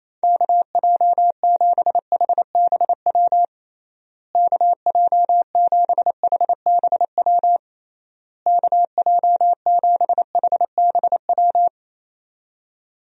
Morse code, heard as a keyed tone, about 28 wpm, 700 Hz. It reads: KJ7HBW KJ7HBW KJ7HBW
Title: USA Call Signs - 6 Characters - Repeated 3x - 28wpm